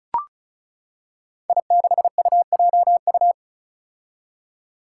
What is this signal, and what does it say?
Morse code, heard as a keyed tone, about 36 wpm, 700 Hz. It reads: I6UJU
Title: Five Challenging Letter-Number Combinations 35wpm